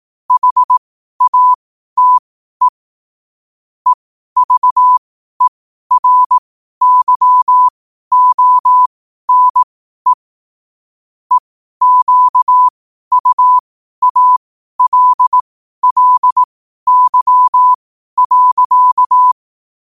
Morse code, heard as a keyed tone, about 18 wpm, spaced as for 12 wpm, 1000 Hz. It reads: HATE EVERYONE EQUALLY.
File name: qotdfiles / 12wpm